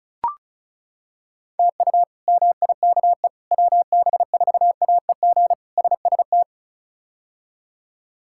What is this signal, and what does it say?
Morse code, heard as a keyed tone, about 34 wpm, 700 Hz. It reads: TU MIKE WB4AEG SST
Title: K1USN SST QSO Practice - 35wpm